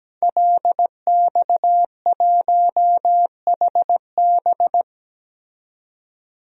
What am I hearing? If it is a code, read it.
Morse code, heard as a keyed tone, about 17 wpm, 700 Hz. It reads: LX1HB
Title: DXCC-100 Call Signs - 5 Characters - 17wpm